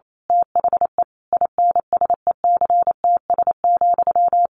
Morse code, heard as a keyed tone, about 28 wpm, 700 Hz. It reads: THE INSECTS,
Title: A Sound Of Thunder 28wpm